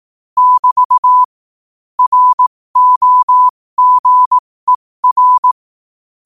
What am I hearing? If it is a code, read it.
Morse code, heard as a keyed tone, about 18 wpm, spaced as for 15 wpm, 1000 Hz. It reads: <BT> ROGER
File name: qotdfiles / 15wpm